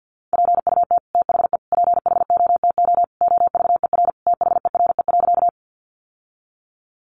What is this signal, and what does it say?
Morse code, heard as a keyed tone, about 48 wpm, 700 Hz. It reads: PUT THE PHOTO OVER THERE.
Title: Sentences from Top 1200 Words 50wpm